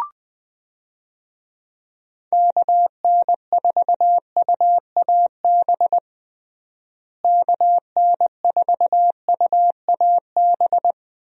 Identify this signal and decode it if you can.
Morse code, heard as a keyed tone, about 21 wpm, 700 Hz. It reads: KN4UAB KN4UAB
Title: USA Call Signs - 6 Characters - Repeated 2x - 20wpm